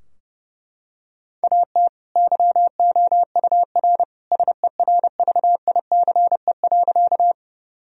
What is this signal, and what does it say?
Morse code, heard as a keyed tone, about 30 wpm, 700 Hz. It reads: AT YOUR SERVICE.
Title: Sentences from Top 1000 Words - Review Part 4 - 30wpm